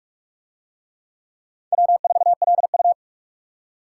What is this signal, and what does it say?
Morse code, heard as a keyed tone, about 39 wpm, 700 Hz. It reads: W4LU